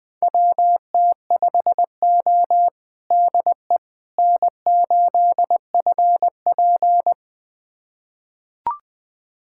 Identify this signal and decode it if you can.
Morse code, heard as a keyed tone, about 21 wpm, 700 Hz. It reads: WT5O DE N8FP
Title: USA Callsign DE Callsign - 4 Characters - 20wpm